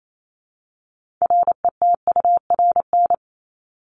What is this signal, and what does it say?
Morse code, heard as a keyed tone, about 28 wpm, 700 Hz. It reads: RETURN
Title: Top 751-1000 Words 28wpm